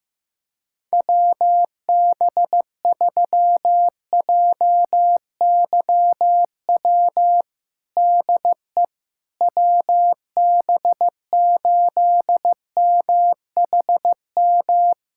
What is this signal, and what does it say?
Morse code, heard as a keyed tone, about 15 wpm, 700 Hz. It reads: WB3JYW DE WB8MHM
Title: USA Callsign DE Callsign - 6 Characters - 15wpm